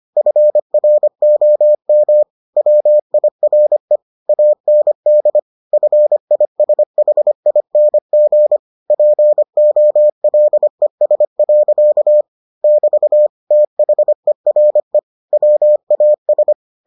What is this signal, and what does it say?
Morse code, heard as a keyed tone, about 25 wpm, 600 Hz. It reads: FROM WIRE AND FISHING POLES. <BT> THERE WAS